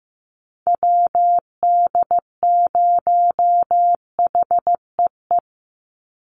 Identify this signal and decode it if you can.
Morse code, heard as a keyed tone, about 15 wpm, 700 Hz. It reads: WD0HEE
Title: USA Call Signs 15wpm